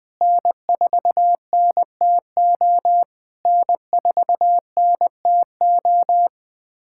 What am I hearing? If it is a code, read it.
Morse code, heard as a keyed tone, about 20 wpm, 700 Hz. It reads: N4NTO N4NTO
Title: POTA QSO Practice - 20wpm